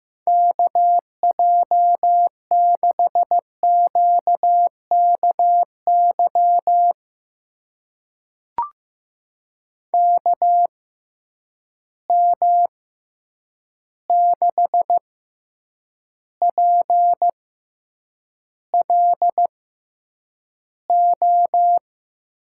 Morse code, heard as a keyed tone, about 15 wpm, spaced as for 8 wpm, 700 Hz. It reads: KJ6QKY KM6PLO
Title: USA Call Signs - 6 Characters - 6x Character Spacing - 15wpm